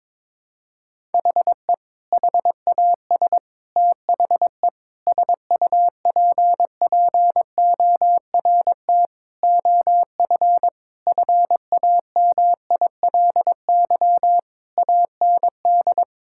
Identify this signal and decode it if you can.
Morse code, heard as a keyed tone, about 22 wpm, 700 Hz. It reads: HE HAS THE SUPPORT OF FAMILY AND
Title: Sentences from Top 1000 Words - Review Part 5 - 22wpm